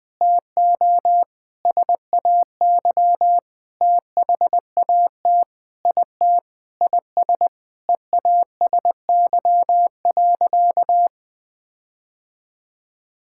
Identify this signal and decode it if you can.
Morse code, heard as a keyed tone, about 21 wpm, 700 Hz. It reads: TO SAY THAT IT IS EASY.
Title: Sentences with 7-Word Phrases - Spoken Beginning - 20wpm